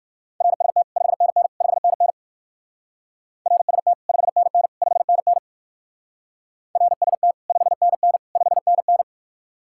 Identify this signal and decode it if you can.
Morse code, heard as a keyed tone, about 55 wpm, 700 Hz. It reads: RST 5NN 5NN RST 5NN 5NN RST 5NN 5NN
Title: Speed-Racing Common QSO Phrases - 45wpm